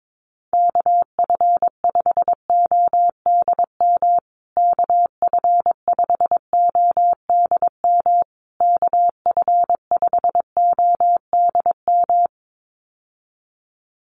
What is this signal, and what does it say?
Morse code, heard as a keyed tone, about 22 wpm, 700 Hz. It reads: KF5ODM KF5ODM KF5ODM
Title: USA Call Signs Repeated 3x 22wpm